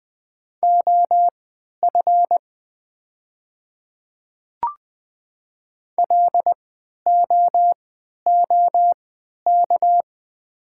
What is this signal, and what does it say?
Morse code, heard as a keyed tone, about 20 wpm, spaced as for 11 wpm, 700 Hz. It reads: OF LOOK
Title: 3x Character Spacing - Top 200 Words - 20wpm